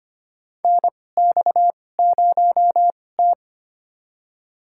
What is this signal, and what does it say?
Morse code, heard as a keyed tone, about 25 wpm, spaced as for 18 wpm, 700 Hz. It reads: NX0T
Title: USA Call Signs - 4 Characters - 2x Character Spacing - 25wpm